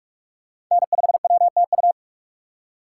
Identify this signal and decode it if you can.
Morse code, heard as a keyed tone, about 44 wpm, 700 Hz. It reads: N5WTU